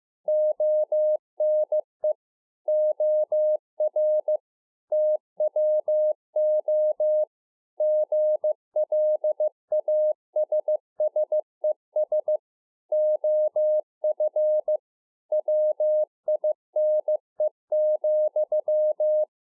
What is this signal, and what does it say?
Morse code, heard as a keyed tone, about 15 wpm, 610 Hz. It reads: ONE OR TWO GLASSES OF WINE,
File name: 15WPM